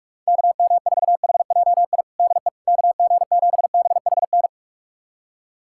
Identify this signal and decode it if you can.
Morse code, heard as a keyed tone, about 42 wpm, 700 Hz. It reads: KM4HJI DE KG7BHN
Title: USA Callsign DE Callsign - 6 Characters - 45wpm